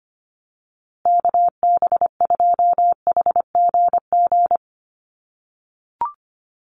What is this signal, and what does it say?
Morse code, heard as a keyed tone, about 25 wpm, 700 Hz. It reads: KB2HGG
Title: USA Call Signs Repeated 2x 25wpm